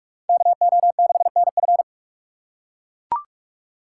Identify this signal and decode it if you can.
Morse code, heard as a keyed tone, about 41 wpm, 700 Hz. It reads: KO6NF